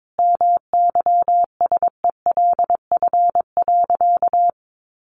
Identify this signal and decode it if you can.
Morse code, heard as a keyed tone, about 22 wpm, 700 Hz. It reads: MYSELF.
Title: Sentences from Top 1300 Words 22wpm